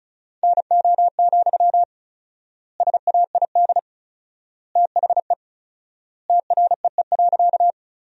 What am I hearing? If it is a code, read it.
Morse code, heard as a keyed tone, about 36 wpm, 700 Hz. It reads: NO, SAID THE TREE.
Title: 4x Word-Spacing - Sentences from Top 300 Words - 35wpm